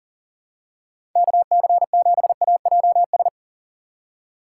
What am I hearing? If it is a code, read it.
Morse code, heard as a keyed tone, about 37 wpm, 700 Hz. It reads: KC7AJS
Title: USA Call Signs - 3x Character Spacing - 40wpm